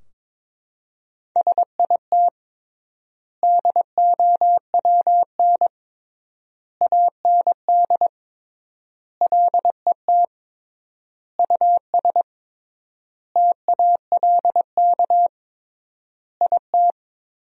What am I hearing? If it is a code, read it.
Morse code, heard as a keyed tone, about 22 wpm, 700 Hz. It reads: SIT DOWN AND LET US TALK IT
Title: 3x Word-Spacing - Sentences from Top 500 Words - 22wpm